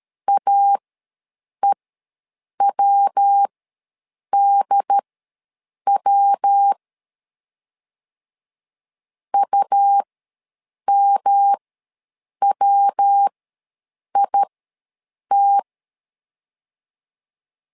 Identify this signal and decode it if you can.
Morse code, heard as a keyed tone, about 13 wpm, spaced as for 7 wpm, 800 Hz. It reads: AEWDW UMWIT
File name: LEZIONI / 3 MWRO / 3 MW ws 12wpm 1816